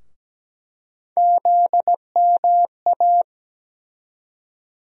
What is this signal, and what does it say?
Morse code, heard as a keyed tone, about 17 wpm, 700 Hz. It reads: ZMA
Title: USA Call Sign Postfixes - 3 Letters 17wpm